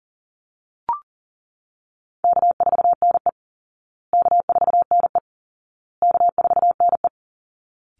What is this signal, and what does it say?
Morse code, heard as a keyed tone, about 40 wpm, 700 Hz. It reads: K4NE K4NE K4NE
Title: Easy Call Signs Repeated 3x - 40wpm